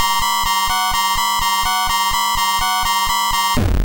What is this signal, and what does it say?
There is no code.